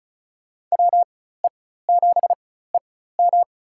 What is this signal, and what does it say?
Morse code, heard as a keyed tone, about 36 wpm, spaced as for 16 wpm, 700 Hz. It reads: WE7EM
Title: USA Call Signs - 5 Characters - 4x Character Spacing - 35wpm